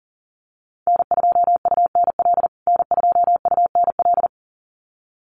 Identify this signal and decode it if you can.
Morse code, heard as a keyed tone, about 40 wpm, 700 Hz. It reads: N2UNL N2UNL
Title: USA Call Signs Repeated 2x 40wpm